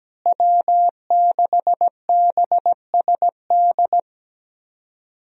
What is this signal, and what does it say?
Morse code, heard as a keyed tone, about 17 wpm, 700 Hz. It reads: W6BSD
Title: USA Call Signs - 5 Characters - Repeated 3x - 17wpm